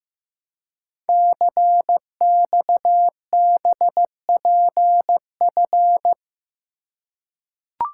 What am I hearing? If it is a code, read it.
Morse code, heard as a keyed tone, about 15 wpm, 700 Hz. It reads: CXBPF
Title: Code Groups - 5 Letters - 15wpm